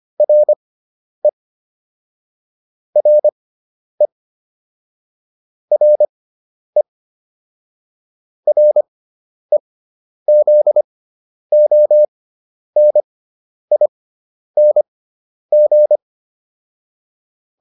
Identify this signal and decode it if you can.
Morse code, heard as a keyed tone, about 25 wpm, spaced as for 10 wpm, 600 Hz. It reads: RE RE RE REZONING